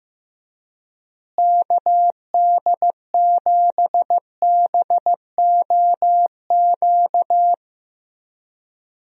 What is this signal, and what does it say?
Morse code, heard as a keyed tone, about 15 wpm, 700 Hz. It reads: KD7BOQ
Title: USA Call Signs - 6 Characters - Repeated 3x - 15wpm